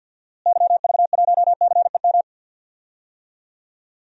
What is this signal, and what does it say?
Morse code, heard as a keyed tone, about 52 wpm, 700 Hz. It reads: YV1CEM
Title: DXCC-100 Call Signs - 6 Characters - Repeated 3x - 50wpm